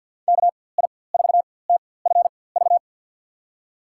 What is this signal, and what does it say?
Morse code, heard as a keyed tone, about 47 wpm, spaced as for 23 wpm, 700 Hz. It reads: KI4TFV
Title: USA Call Signs - 4x Character Spacing - 50wpm